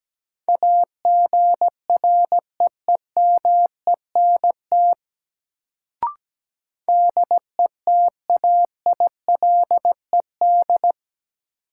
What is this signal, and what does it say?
Morse code, heard as a keyed tone, about 17 wpm, 700 Hz. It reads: AGREEMENT DETAILED
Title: Sentences from Top 1500 Words - New Word Drill - 17wpm